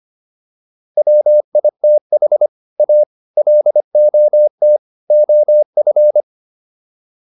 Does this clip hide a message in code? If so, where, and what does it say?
Yes: Morse code from 1.0 to 6.2 s, heard as a keyed tone, about 25 wpm, 600 Hz. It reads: WITH A LOT OF